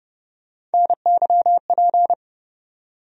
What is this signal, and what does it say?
Morse code, heard as a keyed tone, about 30 wpm, 700 Hz. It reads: NYP
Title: USA Call Sign Postfixes - 3 Letters 30wpm